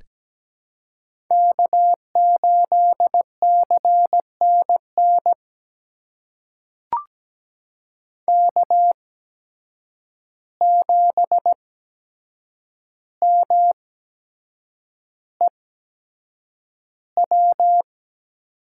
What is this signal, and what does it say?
Morse code, heard as a keyed tone, about 17 wpm, spaced as for 7 wpm, 700 Hz. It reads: K8CNN K7MEW
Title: USA Call Signs - 5 Characters - 8x Character Spacing - 17wpm